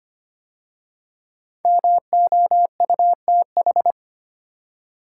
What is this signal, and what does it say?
Morse code, heard as a keyed tone, about 25 wpm, 700 Hz. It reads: MOUTH
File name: 5 Letter Words - First 3 Letters Given - 25wpm